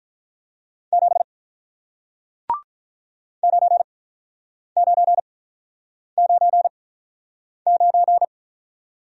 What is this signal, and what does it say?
Morse code, heard as a keyed tone, about 45 wpm, spaced as for 9 wpm, 700 Hz. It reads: 7 9999